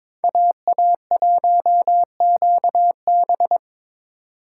Morse code, heard as a keyed tone, about 26 wpm, 700 Hz. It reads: AA1QB
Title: Speed Racing - USA 2x2 Call Signs 22wpm